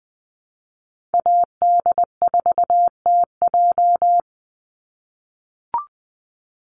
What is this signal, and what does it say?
Morse code, heard as a keyed tone, about 20 wpm, 700 Hz. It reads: AD4TJ
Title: Call Signs - 20wpm